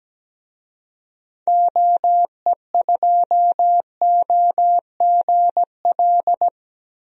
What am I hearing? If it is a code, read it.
Morse code, heard as a keyed tone, about 17 wpm, 700 Hz. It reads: OE2OGL
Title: DXCC-100 Call Signs - 6 Characters - Repeated 3x - 17wpm